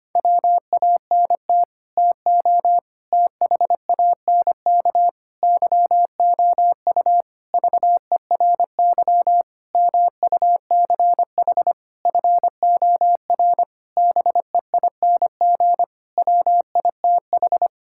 Morse code, heard as a keyed tone, about 25 wpm, 700 Hz. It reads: WANT TO THANK YOU VERY MUCH FOR BEING WITH
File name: Sets of 9 Words - 25wpm